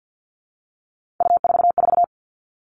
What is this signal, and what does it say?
Morse code, heard as a keyed tone, about 50 wpm, 700 Hz. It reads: U44
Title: All Cut Numbers - 3-Digit 50wpm